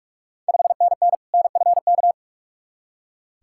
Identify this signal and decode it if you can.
Morse code, heard as a keyed tone, about 44 wpm, 690 Hz. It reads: 5NN NFK
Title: Ontario QSO Party Exchange 45wpm